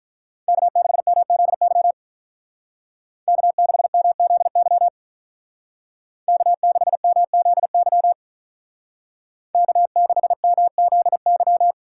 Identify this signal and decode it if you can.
Morse code, heard as a keyed tone, about 46 wpm, 700 Hz. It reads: K6MZY K6MZY K6MZY K6MZY